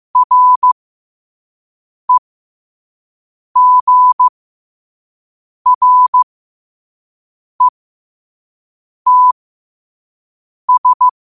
Morse code, heard as a keyed tone, about 15 wpm, spaced as for 5 wpm, 1000 Hz. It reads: REGRETS
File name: qotdfiles / 5wpm